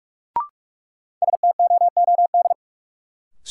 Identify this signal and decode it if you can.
Morse code, heard as a keyed tone, about 43 wpm, 700 Hz. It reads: STOOD
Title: Sentences from Top 500 Words - New Word Drill - 45wpm